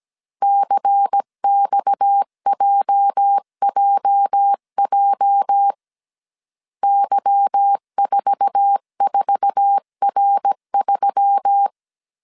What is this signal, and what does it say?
Morse code, heard as a keyed tone, about 17 wpm, 800 Hz. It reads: CXJJJ Y44R3
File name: 1500C S B 16wpm